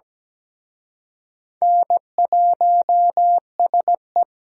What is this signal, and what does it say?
Morse code, heard as a keyed tone, about 20 wpm, 700 Hz. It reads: N1SE
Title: Speed Racing - USA 1x2 Call Signs 17wpm